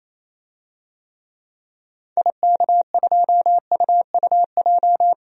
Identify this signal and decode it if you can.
Morse code, heard as a keyed tone, about 28 wpm, 700 Hz. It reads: IK2UUJ
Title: DXCC-100 Call Signs - 6 Characters - 28wpm